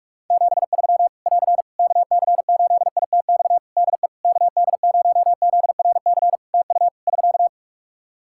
Morse code, heard as a keyed tone, about 48 wpm, 700 Hz. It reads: 73 <AR> KC8ITX DE KD0ZRC TU <SK>
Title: Practice QSO - Exchange 4 - Ending - 45wpm